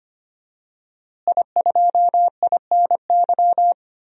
Morse code, heard as a keyed tone, about 25 wpm, 700 Hz. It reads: I2INY
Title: DXCC-100 Call Signs - 5 Characters - Repeated 3x - 25wpm